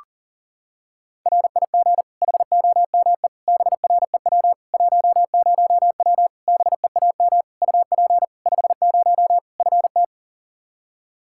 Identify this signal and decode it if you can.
Morse code, heard as a keyed tone, about 40 wpm, 700 Hz. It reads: RIG HOME BREW 10W BEAM UP 50 FT
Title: Practice QSO - Exchange 2 - Info - 40wpm